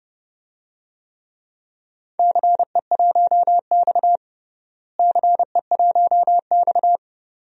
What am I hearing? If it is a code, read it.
Morse code, heard as a keyed tone, about 32 wpm, 700 Hz. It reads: CE1X CE1X
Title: DXCC-100 Call Signs - 4 Characters - Repeated 2x - 30wpm